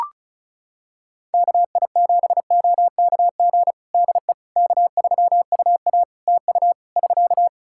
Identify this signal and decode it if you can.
Morse code, heard as a keyed tone, about 36 wpm, 700 Hz. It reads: KI7OKG DE K3UA TU <SK>
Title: Practice QSO - Exchange 4 - Ending - 35wpm